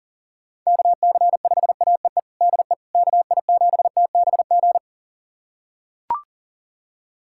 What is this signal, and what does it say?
Morse code, heard as a keyed tone, about 37 wpm, 700 Hz. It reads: KC5AEE DE KI7TBG
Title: USA Callsign DE Callsign - 6 Characters - 40wpm